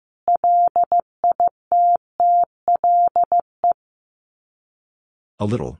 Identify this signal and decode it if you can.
Morse code, heard as a keyed tone, about 15 wpm, 700 Hz. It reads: LITTLE
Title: Sets of 2 Words 15wpm